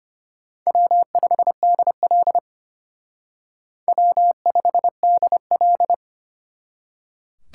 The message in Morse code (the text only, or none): W5DL W5DL